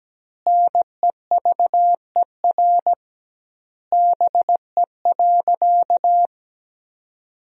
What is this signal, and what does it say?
Morse code, heard as a keyed tone, about 17 wpm, 700 Hz. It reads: NEVER BE.
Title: 2x Word-Spacing - Sentences from Top 300 Words - 17wpm